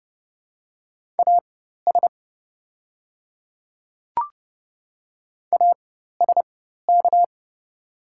Morse code, heard as a keyed tone, about 30 wpm, spaced as for 14 wpm, 700 Hz. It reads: AS ASK
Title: 4x Character Spacing - Top 200 Words - 30wpm